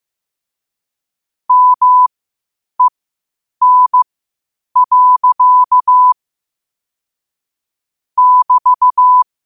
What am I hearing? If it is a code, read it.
Morse code, heard as a keyed tone, about 15 wpm, spaced as for 8 wpm, 1000 Hz. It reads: MEN. <BT>
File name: qotdfiles / 8wpm